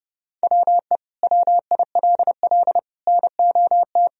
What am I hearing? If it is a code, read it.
Morse code, heard as a keyed tone, about 30 wpm, 700 Hz. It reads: WE WILL NOT
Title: Sentences from Top 1000 Words - Review Part 2 - 30wpm